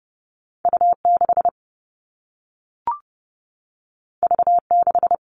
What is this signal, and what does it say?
Morse code, heard as a keyed tone, about 30 wpm, 700 Hz. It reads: U6 V6
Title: All Cut Numbers - 2-Digit 30wpm